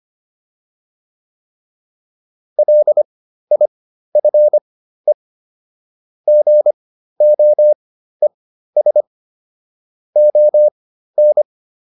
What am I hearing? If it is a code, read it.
Morse code, heard as a keyed tone, about 25 wpm, spaced as for 13 wpm, 600 Hz. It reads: LIFE GOES ON